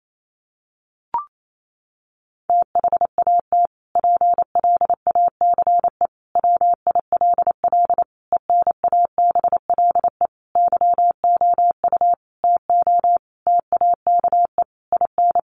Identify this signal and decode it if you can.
Morse code, heard as a keyed tone, about 28 wpm, 700 Hz. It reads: THAT PLACE WILL ENABLE YOU TO TAKE IN
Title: Sentences from Top 1500 Words 28wpm